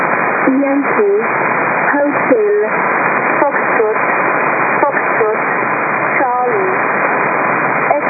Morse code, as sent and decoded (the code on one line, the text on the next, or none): none